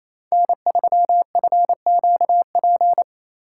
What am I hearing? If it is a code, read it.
Morse code, heard as a keyed tone, about 28 wpm, 700 Hz. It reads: N3FQP
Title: USA Call Signs - 5 Characters - 4x Character Spacing - 28wpm